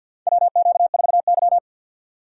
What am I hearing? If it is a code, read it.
Morse code, heard as a keyed tone, about 52 wpm, 700 Hz. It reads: WQ4Y
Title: USA Call Signs - 4 Characters - 6x Character Spacing - 50wpm